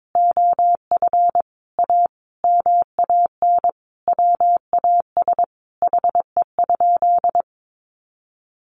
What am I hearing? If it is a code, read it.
Morse code, heard as a keyed tone, about 22 wpm, 700 Hz. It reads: OF A MAN WAS HE?
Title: Sentences from Top 1300 Words 22wpm